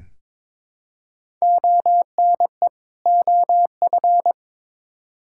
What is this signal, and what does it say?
Morse code, heard as a keyed tone, about 22 wpm, 700 Hz. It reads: ONE OF
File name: practice / 22wpm / Build A Phrase - 5-Word Phrases - 22wpm